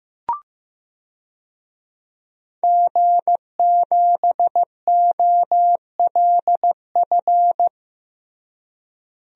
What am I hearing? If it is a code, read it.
Morse code, heard as a keyed tone, about 15 wpm, 700 Hz. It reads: G7OLF
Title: DXCC-100 Call Signs - 5 Characters - Repeated 2x - 15wpm